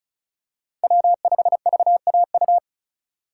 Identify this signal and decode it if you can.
Morse code, heard as a keyed tone, about 36 wpm, 700 Hz. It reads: W5VAU
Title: USA Call Signs - 5 Characters - 2x Character Spacing - 35wpm